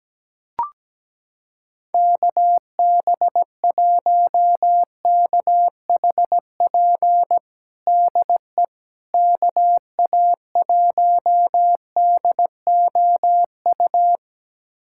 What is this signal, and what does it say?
Morse code, heard as a keyed tone, about 17 wpm, 700 Hz. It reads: KB1KHP DE KA1DOU
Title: USA Callsign DE Callsign 17wpm